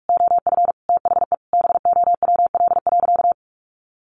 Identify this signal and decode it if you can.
Morse code, heard as a keyed tone, about 44 wpm, 700 Hz. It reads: OF THE BOWL.
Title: Sentences from Top 1200 Words 45wpm